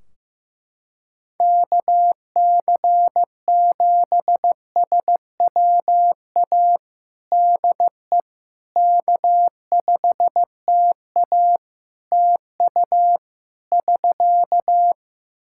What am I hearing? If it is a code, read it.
Morse code, heard as a keyed tone, about 15 wpm, 700 Hz. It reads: KC7SWA DE K5TA TU <SK>
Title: Practice QSO - Exchange 4 - Ending - 15wpm